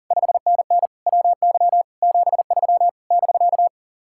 Morse code, heard as a keyed tone, about 48 wpm, 700 Hz. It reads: 5NN WY 73 <BK>
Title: Speed-Racing Common QSO Phrases Encore - 40wpm